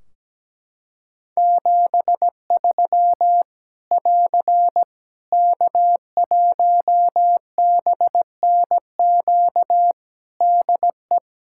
Morse code, heard as a keyed tone, about 17 wpm, 700 Hz. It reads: 73 <AR> K1BNQ DE
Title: Practice QSO - Exchange 4 - Ending - 17wpm